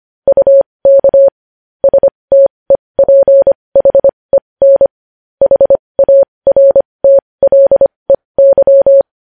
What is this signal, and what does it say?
Morse code, heard as a keyed tone, about 25 wpm, spaced as for 20 wpm, 550 Hz. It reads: UK STEPHEN HARTLEY
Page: News headlines in morse code at 20 wpm